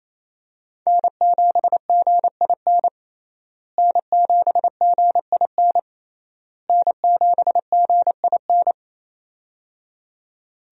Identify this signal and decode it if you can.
Morse code, heard as a keyed tone, about 28 wpm, 700 Hz. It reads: N7GIN N7GIN N7GIN
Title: USA Call Signs - 5 Characters - Repeated 3x - 28wpm